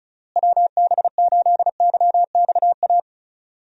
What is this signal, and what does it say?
Morse code, heard as a keyed tone, about 34 wpm, 700 Hz. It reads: WB8YXA
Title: USA Call Signs - 6 Characters - Repeated 2x - 35wpm